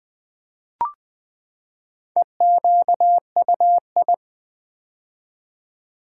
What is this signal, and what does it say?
Morse code, heard as a keyed tone, about 20 wpm, 700 Hz. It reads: EQUI
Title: Build A Word - 4 Letters At A Time - 10-Letter Words - 20wpm